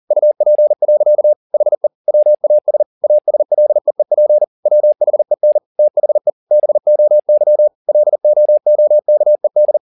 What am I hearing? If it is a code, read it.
Morse code, heard as a keyed tone, about 40 wpm, 600 Hz. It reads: UP. HE WAS ASLEEP WHEN THE BOY LOOKED
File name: the old man and the sea 2 40wpm